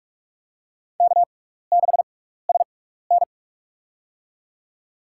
Morse code, heard as a keyed tone, about 46 wpm, spaced as for 16 wpm, 700 Hz. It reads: K6SN
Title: USA Call Signs - 4 Characters - 6x Character Spacing - 45wpm